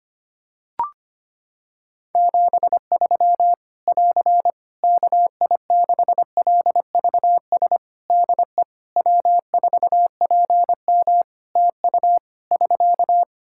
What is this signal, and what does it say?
Morse code, heard as a keyed tone, about 25 wpm, 700 Hz. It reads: 73 <AR> KI6LVS DE W4PM TU <SK>
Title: Practice QSO - Exchange 4 - Ending - 25wpm